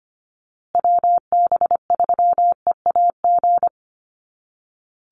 Morse code, heard as a keyed tone, about 25 wpm, 700 Hz. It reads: WB3EAG
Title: USA Call Signs Repeated 3x 25wpm